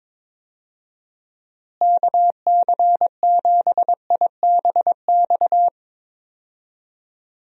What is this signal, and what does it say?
Morse code, heard as a keyed tone, about 22 wpm, 700 Hz. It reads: KC7IBX
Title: USA Call Signs - 6 Characters - Repeated 3x - 22wpm